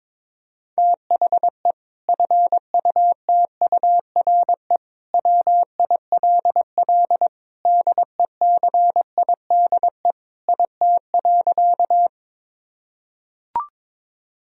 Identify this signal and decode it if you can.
Morse code, heard as a keyed tone, about 22 wpm, 700 Hz. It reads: THE FUTURE WILL DECIDE IT.
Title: Sentences from Top 1000 Words - Review Part 3 - 22wpm